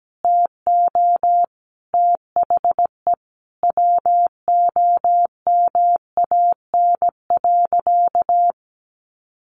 Morse code, heard as a keyed tone, about 17 wpm, 700 Hz. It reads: TO THE WOMAN.
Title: Sentences from Top 1400 Words 17wpm